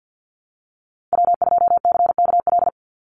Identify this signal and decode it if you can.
Morse code, heard as a keyed tone, about 50 wpm, 700 Hz. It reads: W2CKL
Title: NA SOTA Activator Callsigns 50wpm